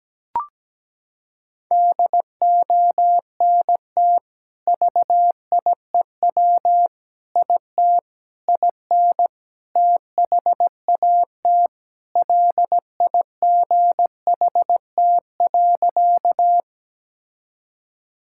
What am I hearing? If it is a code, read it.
Morse code, heard as a keyed tone, about 17 wpm, 700 Hz. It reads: DONT VIEW IT IN THAT LIGHT.
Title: Sentences from Top 1000 Words - Review Part 6 - 17wpm